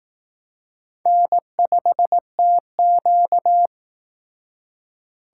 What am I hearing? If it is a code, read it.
Morse code, heard as a keyed tone, about 18 wpm, 700 Hz. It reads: N5TQ
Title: Speed Racing - USA 1x2 Call Signs 15wpm